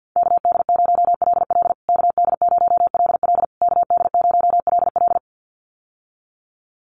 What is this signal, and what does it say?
Morse code, heard as a keyed tone, about 49 wpm, 700 Hz. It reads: KD0LL KD0LL KD0LL
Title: USA Call Signs Repeated 3x 50wpm